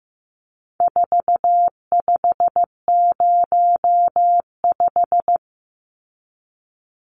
Morse code, heard as a keyed tone, about 15 wpm, 700 Hz. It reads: 4505
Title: Numbers - 4-Digit 15wpm